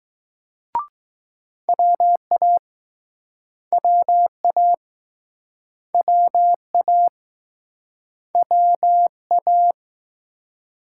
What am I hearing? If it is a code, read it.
Morse code, heard as a keyed tone, about 21 wpm, 700 Hz. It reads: WA WA WA WA